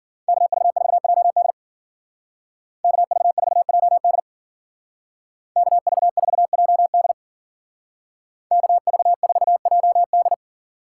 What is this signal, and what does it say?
Morse code, heard as a keyed tone, about 49 wpm, 700 Hz. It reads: KV4JD KV4JD KV4JD KV4JD
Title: Speed Racing - USA 2x2 Call Signs 40wpm